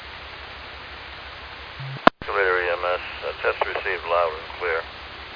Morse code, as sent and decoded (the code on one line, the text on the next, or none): none